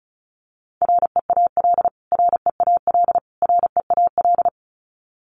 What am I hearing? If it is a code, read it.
Morse code, heard as a keyed tone, about 34 wpm, 700 Hz. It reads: REAL REAL REAL
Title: Top 300 Words Repeated 35wpm